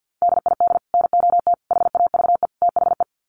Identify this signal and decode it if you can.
Morse code, heard as a keyed tone, about 50 wpm, 700 Hz. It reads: DID NOT HAVE THE
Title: Sets of 4 Words 50wpm